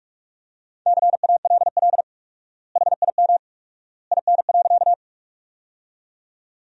Morse code, heard as a keyed tone, about 46 wpm, 700 Hz. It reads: CALL HIM IN.
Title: 4x Word-Spacing - Sentences from Top 100 Words - 45wpm